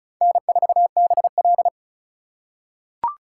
Morse code, heard as a keyed tone, about 34 wpm, 700 Hz. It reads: N4BL
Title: USA Call Signs - 4 Characters - 3x Character Spacing - 35wpm